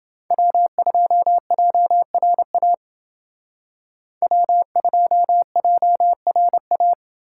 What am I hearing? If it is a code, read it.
Morse code, heard as a keyed tone, about 28 wpm, 700 Hz. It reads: W2JRA W2JRA